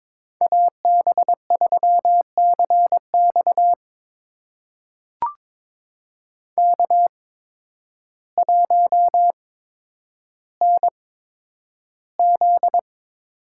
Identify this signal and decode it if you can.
Morse code, heard as a keyed tone, about 22 wpm, spaced as for 10 wpm, 700 Hz. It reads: AB3CX K1NZ
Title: USA Call Signs - 5 Characters - 8x Character Spacing - 22wpm